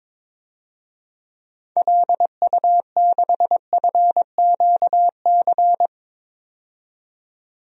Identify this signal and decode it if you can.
Morse code, heard as a keyed tone, about 21 wpm, 700 Hz. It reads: LU6FQC
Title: DXCC-100 Call Signs - 6 Characters - 22wpm